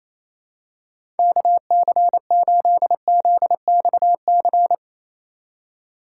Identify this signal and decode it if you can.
Morse code, heard as a keyed tone, about 27 wpm, 700 Hz. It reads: KC8ZXC
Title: USA Call Signs - 6 Characters - 4x Character Spacing - 28wpm